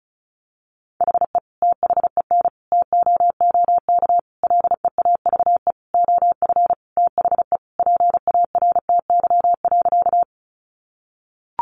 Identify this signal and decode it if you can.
Morse code, heard as a keyed tone, about 34 wpm, 700 Hz. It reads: HE THEN TOOK LEAVE OF THE PARTY.
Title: Sentences from Top 1400 Words 35wpm